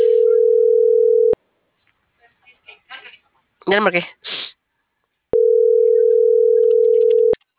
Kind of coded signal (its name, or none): telephone call progress tone